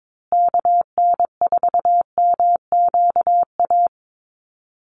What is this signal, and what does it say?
Morse code, heard as a keyed tone, about 22 wpm, 700 Hz. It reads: KN4MQA
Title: USA Call Signs Repeated 3x 22wpm